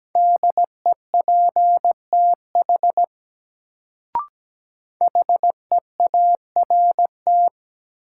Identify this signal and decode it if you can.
Morse code, heard as a keyed tone, about 17 wpm, 700 Hz. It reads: DEPTH HEART
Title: Sentences from Top 1200 Words - New Word Drill - 17wpm